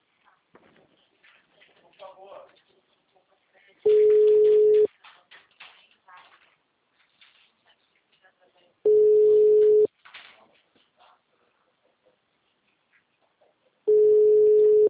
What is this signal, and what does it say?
Ringback tone (Europe, 425 Hz), with speech.